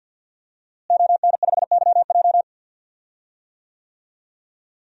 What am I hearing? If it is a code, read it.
Morse code, heard as a keyed tone, about 52 wpm, 700 Hz. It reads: ON5YJ
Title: DXCC-100 Call Signs - 5 Characters - Repeated 3x - 50wpm